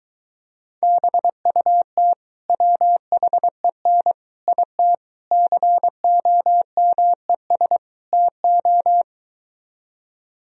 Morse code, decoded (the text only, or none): BUT WHEN IT COMES TO